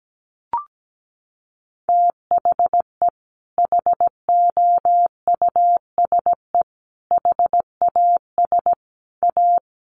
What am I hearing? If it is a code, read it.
Morse code, heard as a keyed tone, about 17 wpm, 700 Hz. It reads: THE HOUSE HAS A
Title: Sentences from Top 1200 Words 17wpm